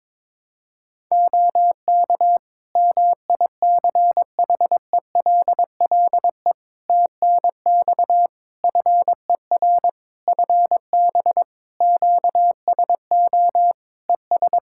Morse code, heard as a keyed tone, about 22 wpm, 700 Hz. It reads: OK MICHELLE TNX FER FB QSO ES
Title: Practice QSO - Exchange 4 - Ending - 22wpm